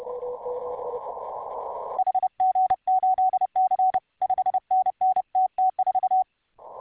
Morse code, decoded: HG8C 5NN TT4

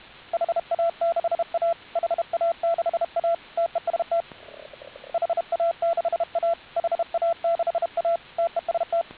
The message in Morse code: HA6A HA6A TEST HA6A HA6A TEST

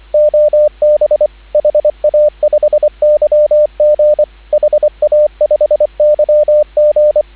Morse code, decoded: OB HA5YG HA5YG

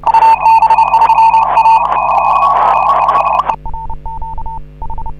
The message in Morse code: HA5MY TES

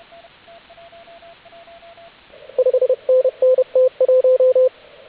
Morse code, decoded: 5NN T1

